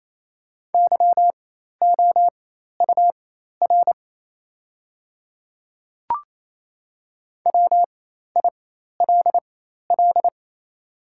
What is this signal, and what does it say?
Morse code, heard as a keyed tone, about 28 wpm, spaced as for 13 wpm, 700 Hz. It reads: YOUR WILL
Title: 4x Character Spacing - Top 100 Words - 28wpm